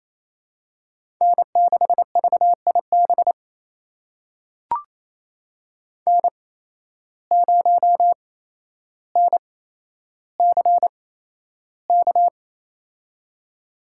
Morse code, heard as a keyed tone, about 28 wpm, spaced as for 12 wpm, 700 Hz. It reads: N6VIB N0NCK
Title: USA Call Signs - 5 Characters - 8x Character Spacing - 28wpm